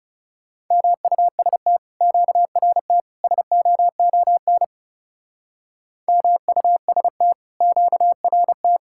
Morse code, heard as a keyed tone, about 36 wpm, 700 Hz. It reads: MUST QRT SOON MUST QRT
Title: Speed-Racing Common QSO Phrases - 30wpm